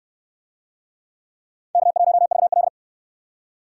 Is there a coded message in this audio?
Yes: Morse code from 1.7 to 2.7 s, heard as a keyed tone, about 69 wpm, 700 Hz. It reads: K1VL